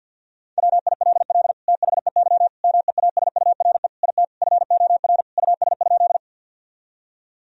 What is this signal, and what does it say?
Morse code, heard as a keyed tone, about 52 wpm, 700 Hz. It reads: WILL THEY MEASURE IT FOR US?
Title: Sentences from Top 1000 Words - Review Part 2 - 50wpm